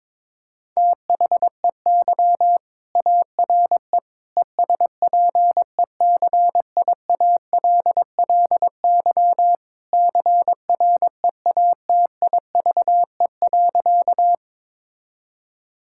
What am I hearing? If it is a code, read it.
Morse code, heard as a keyed tone, about 22 wpm, 700 Hz. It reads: THEY ARE ESPECIALLY CREATIVE.
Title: Sentences from Top 1000 Words - Review Part 5 - 22wpm